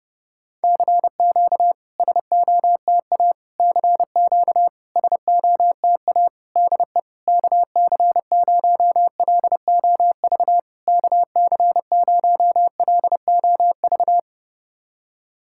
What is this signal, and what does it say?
Morse code, heard as a keyed tone, about 30 wpm, 700 Hz. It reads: CQ SOTA CQ SOTA DE KC0LOV KC0LOV
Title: SOTA QSO Practice 30wpm